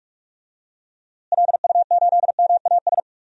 Morse code, heard as a keyed tone, about 46 wpm, 700 Hz. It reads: LU8MAS
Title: DXCC-100 Call Signs - 6 Characters - Repeated 2x - 45wpm